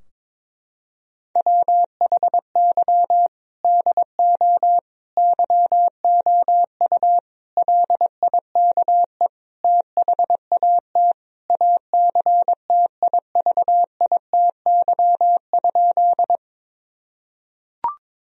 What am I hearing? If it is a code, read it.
Morse code, heard as a keyed tone, about 22 wpm, 700 Hz. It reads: WHY DO YOU LIKE THAT ACTIVITY?
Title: Sentences from Top 1000 Words - Review Part 4 - 22wpm